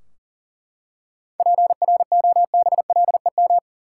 Morse code, heard as a keyed tone, about 40 wpm, 700 Hz. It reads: PROBLEM
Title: Seven Letter Words - 40wpm